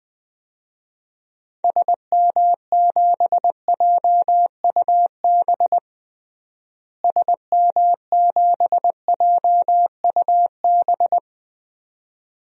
Morse code, heard as a keyed tone, about 20 wpm, 700 Hz. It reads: SM7JUB SM7JUB
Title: DXCC-100 Call Signs - 6 Characters - Repeated 2x - 20wpm